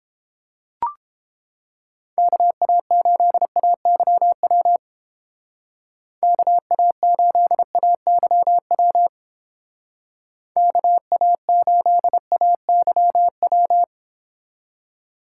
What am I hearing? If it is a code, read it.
Morse code, heard as a keyed tone, about 30 wpm, 700 Hz. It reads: KA8AYW KA8AYW KA8AYW